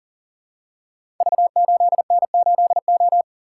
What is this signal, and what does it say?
Morse code, heard as a keyed tone, about 42 wpm, 700 Hz. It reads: V8N8O
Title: Code Groups - 5 Letter:Numbers - 40wpm